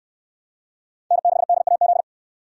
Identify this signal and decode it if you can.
Morse code, heard as a keyed tone, about 58 wpm, 700 Hz. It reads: N6DAZ